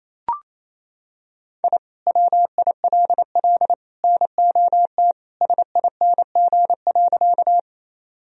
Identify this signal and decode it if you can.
Morse code, heard as a keyed tone, about 28 wpm, 700 Hz. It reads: I WILL NOT SING.
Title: Sentences from Top 1000 Words - Review Part 2 - 28wpm